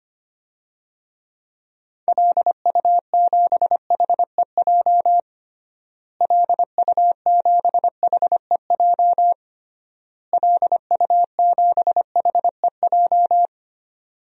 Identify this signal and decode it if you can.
Morse code, heard as a keyed tone, about 25 wpm, 700 Hz. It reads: LU7HEJ LU7HEJ LU7HEJ
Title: DXCC-100 Call Signs - 6 Characters - Repeated 3x - 25wpm